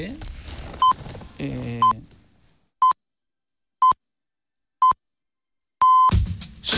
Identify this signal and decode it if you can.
Radio time signal (time pips), with speech. It is heard from 0.8 to 6.1 s.